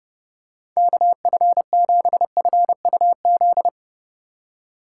Morse code, heard as a keyed tone, about 30 wpm, 700 Hz. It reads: KF7FUZ
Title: USA Call Signs - 6 Characters - 4x Character Spacing - 30wpm